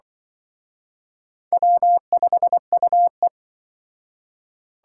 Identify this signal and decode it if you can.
Morse code, heard as a keyed tone, about 24 wpm, 700 Hz. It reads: W5UE
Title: Speed Racing - USA 1x2 Call Signs 20wpm